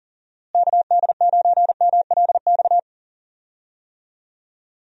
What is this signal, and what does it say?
Morse code, heard as a keyed tone, about 37 wpm, 700 Hz. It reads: KD9MLX
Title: USA Call Signs - 6 Characters - Repeated 3x - 40wpm